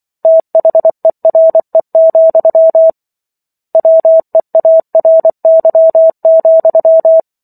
Morse code, heard as a keyed tone, about 24 wpm, 650 Hz. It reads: THERE, WEARY,